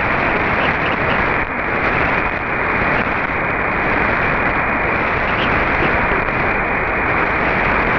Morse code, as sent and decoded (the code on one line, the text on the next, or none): none